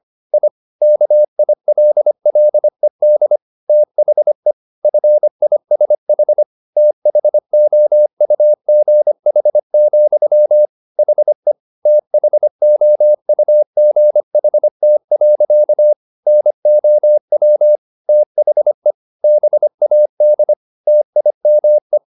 Morse code, heard as a keyed tone, about 25 wpm, 600 Hz. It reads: I KILLED THE FISH THOUGH, HE THOUGHT. NOW THE BAD TIME